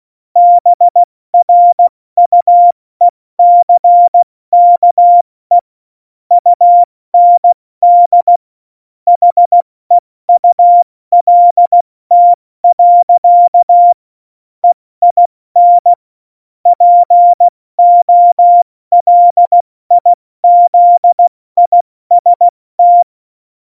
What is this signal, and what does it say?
Morse code, heard as a keyed tone, about 16 wpm, 700 Hz. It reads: BRUECKE UND HEULT. EIN POLIZIST
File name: Morsekurs / Witze / Witz 24. Oktober 17, Tempo 70